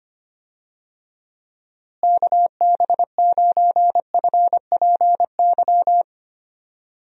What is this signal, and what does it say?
Morse code, heard as a keyed tone, about 25 wpm, 700 Hz. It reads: KB9FPY
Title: USA Call Signs - 6 Characters - Repeated 3x - 25wpm